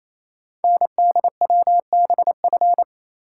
Morse code, heard as a keyed tone, about 29 wpm, 700 Hz. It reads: NDWBF
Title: Code Groups - 5 Letters - 28wpm